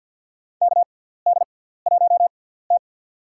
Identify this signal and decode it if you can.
Morse code, heard as a keyed tone, about 53 wpm, spaced as for 17 wpm, 700 Hz. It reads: KD1T
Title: USA Call Signs - 4 Characters - 6x Character Spacing - 50wpm